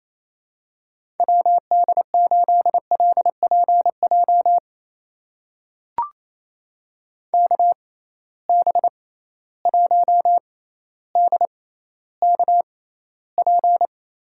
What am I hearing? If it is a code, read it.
Morse code, heard as a keyed tone, about 28 wpm, spaced as for 14 wpm, 700 Hz. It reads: WD8LPJ KB1DKP
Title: USA Call Signs - 6 Characters - 6x Character Spacing - 28wpm